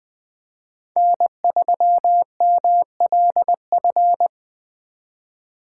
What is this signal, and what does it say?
Morse code, heard as a keyed tone, about 20 wpm, 700 Hz. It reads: N3MLF
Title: USA Call Signs - 5 Characters - 4x Character Spacing - 20wpm